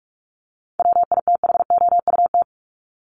Morse code, heard as a keyed tone, about 44 wpm, 700 Hz. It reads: WITHOUT
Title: Top 501-750 Words 45wpm